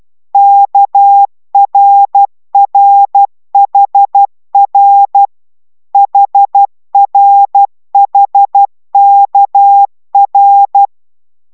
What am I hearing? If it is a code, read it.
Morse code, heard as a keyed tone, about 12 wpm, 800 Hz. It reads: KRRHR HRHKR